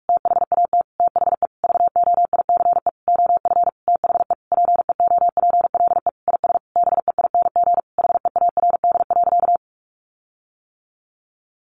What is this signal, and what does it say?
Morse code, heard as a keyed tone, about 44 wpm, 700 Hz. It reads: THAT THE VOICE OF THE PEOPLE IS BEING HEARD.
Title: Sentences from Top 1500 Words 45wpm